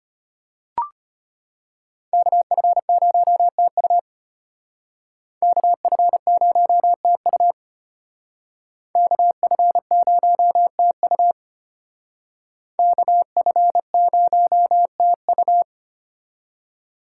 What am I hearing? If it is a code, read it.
Morse code, heard as a keyed tone, about 29 wpm, 700 Hz. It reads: KF0TU KF0TU KF0TU KF0TU
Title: Speed Racing - USA 2x2 Call Signs 25wpm